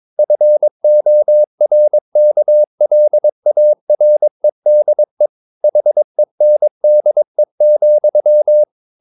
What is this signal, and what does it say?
Morse code, heard as a keyed tone, about 22 wpm, 600 Hz. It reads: FORKLAREDE HENDE,
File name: Ferie QRS 22wpm